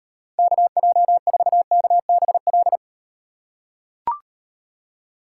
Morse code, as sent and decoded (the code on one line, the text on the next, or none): -.- .--- ....- -.- -... .-..
KJ4KBL